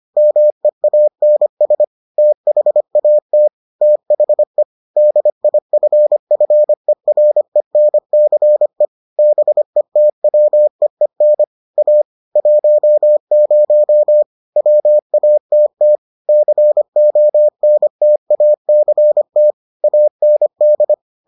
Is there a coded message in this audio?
Yes: Morse code from 0.2 to 20.9 s, heard as a keyed tone, about 25 wpm, 600 Hz. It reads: MEANS THAT THE DIFFERENCE BETWEEN A 10 WATT CONTACT AND